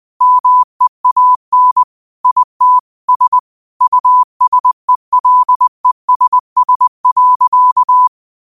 Morse code, heard as a keyed tone, about 20 wpm, 1000 Hz. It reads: MEAN IT S USELESS.
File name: qotdfiles / 20wpm